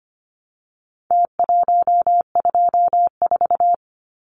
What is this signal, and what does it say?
Morse code, heard as a keyed tone, about 25 wpm, 700 Hz. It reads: T124
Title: Numbers - Contest Style - 4-Digit 25wpm